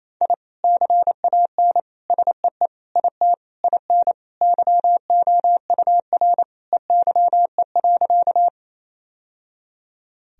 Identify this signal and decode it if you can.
Morse code, heard as a keyed tone, about 27 wpm, 700 Hz. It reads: I CAN SEE IT IN YOUR EYE.
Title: Sentences from Top 1000 Words - Review Part 1 - 28wpm